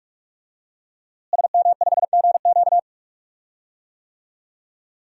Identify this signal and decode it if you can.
Morse code, heard as a keyed tone, about 47 wpm, 700 Hz. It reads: SM5GQ
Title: DXCC-100 Call Signs - 5 Characters - Repeated 3x - 45wpm